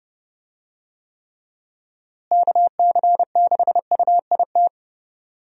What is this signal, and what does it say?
Morse code, heard as a keyed tone, about 30 wpm, 700 Hz. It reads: KC6UIT
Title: USA Call Signs - 6 Characters - Repeated 2x - 30wpm